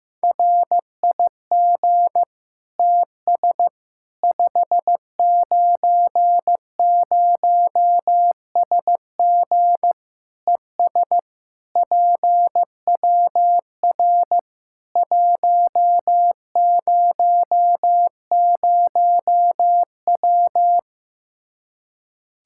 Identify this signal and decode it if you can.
Morse code, heard as a keyed tone, about 15 wpm, 700 Hz. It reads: RIG TS 590SG ES PWR 100W
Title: Practice QSO - Exchange 2 - Info - 15wpm